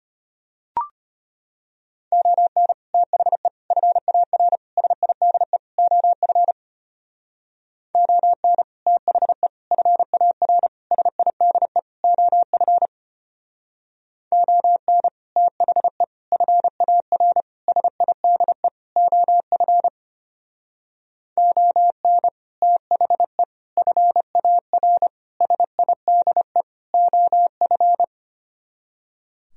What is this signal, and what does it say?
Morse code, heard as a keyed tone, about 31 wpm, 700 Hz. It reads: ON THE FAR SIDE OF ON THE FAR SIDE OF ON THE FAR SIDE OF ON THE FAR SIDE OF
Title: Speed-Racing - Sets of 5 Words 25wpm